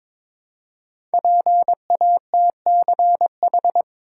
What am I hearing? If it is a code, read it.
Morse code, heard as a keyed tone, about 22 wpm, 700 Hz. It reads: PATCH
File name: 5 Letter Words - First 3 Letters Given - 22wpm